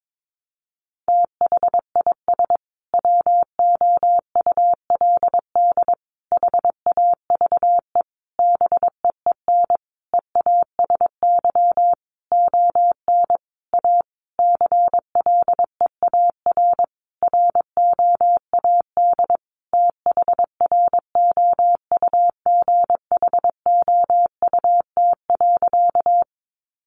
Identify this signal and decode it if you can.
Morse code, heard as a keyed tone, about 22 wpm, 700 Hz. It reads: THIS WOULD HAVE BEEN EASY ON A CLEAR ROAD THROUGHOUT.
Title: Sentences from Top 800 Words 22wpm